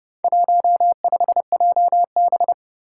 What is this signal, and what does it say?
Morse code, heard as a keyed tone, about 30 wpm, 700 Hz. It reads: 15JB
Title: Four Challenging Letter-Number Combinations 30wpm